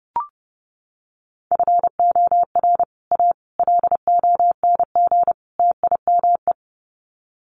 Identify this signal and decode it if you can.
Morse code, heard as a keyed tone, about 30 wpm, 700 Hz. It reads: FOR A LONG TIME
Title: Sets of 4 Words 30wpm